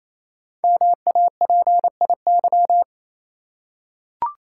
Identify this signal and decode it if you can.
Morse code, heard as a keyed tone, about 29 wpm, 700 Hz. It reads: MAPIY
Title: Code Groups - 5 Letters - 28wpm